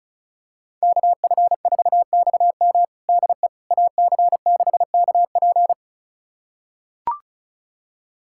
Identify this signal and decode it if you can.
Morse code, heard as a keyed tone, about 34 wpm, 700 Hz. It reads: KF4XM DE AC6KP
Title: USA Callsign DE Callsign - 5 Characters - 35wpm